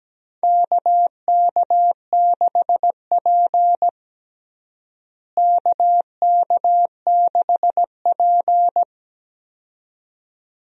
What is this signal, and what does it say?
Morse code, heard as a keyed tone, about 17 wpm, 700 Hz. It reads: KK6P KK6P
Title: USA Call Signs - 4 Characters - Repeated 2x - 17wpm